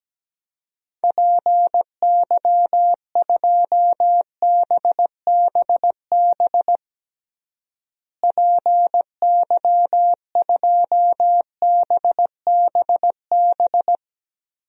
Morse code, heard as a keyed tone, about 17 wpm, 700 Hz. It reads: PY2BBB PY2BBB
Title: DXCC-100 Call Signs - 6 Characters - Repeated 2x - 17wpm